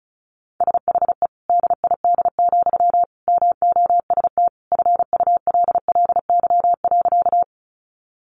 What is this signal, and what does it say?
Morse code, heard as a keyed tone, about 34 wpm, 700 Hz. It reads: SHE DID, MOST FULLY.
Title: Sentences from Top 1300 Words 35wpm